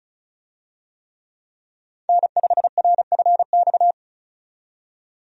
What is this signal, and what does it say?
Morse code, heard as a keyed tone, about 36 wpm, 700 Hz. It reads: N5RFX
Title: USA Call Signs - 5 Characters - 35wpm